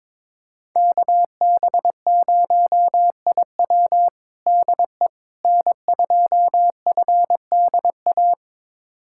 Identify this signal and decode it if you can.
Morse code, heard as a keyed tone, about 22 wpm, 700 Hz. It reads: KB0IW DE N2FDA
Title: USA Callsign DE Callsign 22wpm